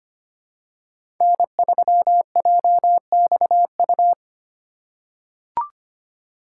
Morse code, heard as a keyed tone, about 25 wpm, 700 Hz. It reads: N3JXU
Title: USA Call Signs - 5 Characters - 4x Character Spacing - 25wpm